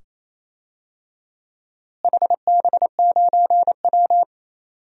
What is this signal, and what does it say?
Morse code, heard as a keyed tone, about 28 wpm, 700 Hz. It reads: HB9W
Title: DXCC-100 Call Signs - 4 Characters - Repeated 2x - 28wpm